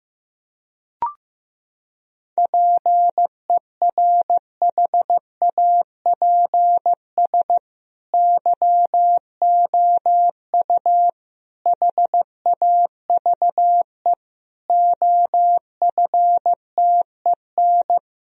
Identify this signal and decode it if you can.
Morse code, heard as a keyed tone, about 15 wpm, 700 Hz. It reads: PERHAPS YOU HAVE OFTEN
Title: Sentences from Top 1000 Words - Review Part 1 - 15wpm